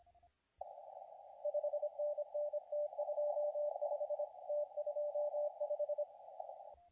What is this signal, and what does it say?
Morse code, heard as a keyed tone, about 25 wpm, 590 Hz, with radio noise. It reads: 5NN T25 T25
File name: la4c / sac-cw-2017 / CONTEST AUDIO 2100Z